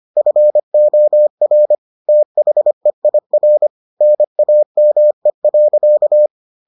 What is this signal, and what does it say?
Morse code, heard as a keyed tone, about 25 wpm, 600 Hz. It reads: FOR THEIR NAME.